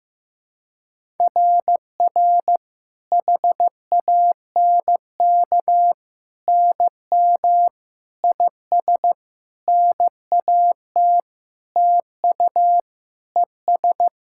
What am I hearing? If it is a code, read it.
Morse code, heard as a keyed tone, about 15 wpm, 700 Hz. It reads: RR HANK NM IS NAT TU ES
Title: SOTA QSO Practice 15wpm